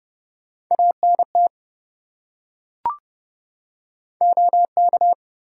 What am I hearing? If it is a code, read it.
Morse code, heard as a keyed tone, about 27 wpm, 700 Hz. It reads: ANT OK